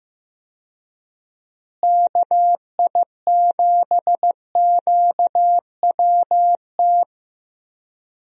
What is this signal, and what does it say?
Morse code, heard as a keyed tone, about 15 wpm, 700 Hz. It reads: KI7QWT
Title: USA Call Signs - 6 Characters - Repeated 2x - 15wpm